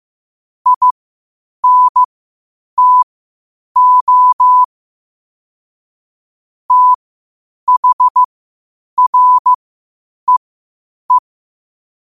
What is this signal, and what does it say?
Morse code, heard as a keyed tone, about 15 wpm, spaced as for 8 wpm, 1000 Hz. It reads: INTO THREE
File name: qotdfiles / 8wpm